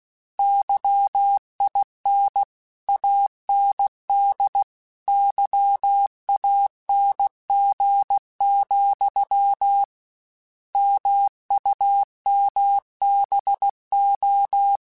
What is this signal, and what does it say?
Morse code, heard as a keyed tone, about 16 wpm, 800 Hz. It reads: YIN AND YANG, MUMBO
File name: Misc / Misc-mp3s / Word Pairs @15